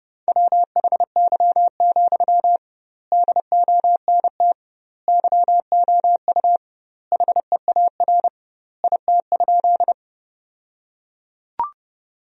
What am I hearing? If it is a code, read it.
Morse code, heard as a keyed tone, about 32 wpm, 700 Hz. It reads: WHY, DONT YOU HEAR IT?
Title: 2x Word-Spacing - Sentences from Top 300 Words - 30wpm